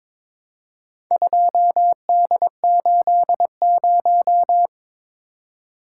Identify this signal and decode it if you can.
Morse code, heard as a keyed tone, about 22 wpm, 700 Hz. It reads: 2D80
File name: practice / 22wpm / Four Letter-Number Combinations - 22wpm